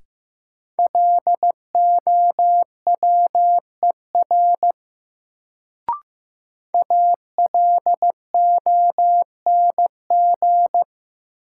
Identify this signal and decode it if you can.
Morse code, heard as a keyed tone, about 15 wpm, 700 Hz. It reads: LOWER ALONG
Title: Sentences from Top 700 Words - New Word Drill - 15wpm